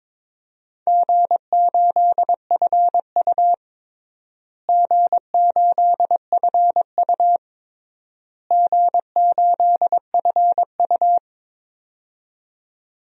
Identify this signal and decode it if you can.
Morse code, heard as a keyed tone, about 22 wpm, 700 Hz. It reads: G8FU G8FU G8FU
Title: DXCC-100 Call Signs - 4 Characters - Repeated 3x - 22wpm